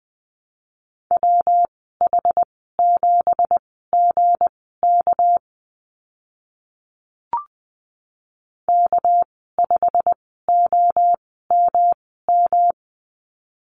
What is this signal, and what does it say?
Morse code, heard as a keyed tone, about 20 wpm, spaced as for 15 wpm, 700 Hz. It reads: WH7GK K5OMM
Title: 213 Lesson 34g - Number 8 - Call Signs 20wpm